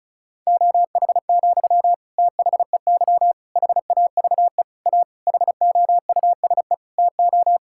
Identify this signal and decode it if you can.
Morse code, heard as a keyed tone, about 34 wpm, 700 Hz. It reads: OH, THEY HAVE A HOUSE TO
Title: Sentences from Top 1000 Words - Review Part 5 - 35wpm